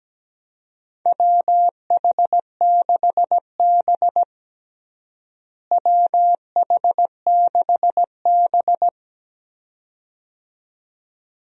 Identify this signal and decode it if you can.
Morse code, heard as a keyed tone, about 17 wpm, 700 Hz. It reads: WH6B WH6B
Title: USA Call Signs - 4 Characters - Repeated 2x - 17wpm